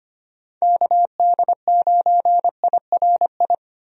Morse code, heard as a keyed tone, about 25 wpm, 700 Hz. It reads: KD9IRI
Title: USA Call Signs - 6 Characters - 25wpm